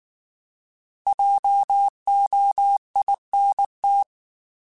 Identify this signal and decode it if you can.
Morse code, heard as a keyed tone, about 19 wpm, 800 Hz. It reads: JOINT
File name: Alphabetical / Alphabetical mp3s / Js @18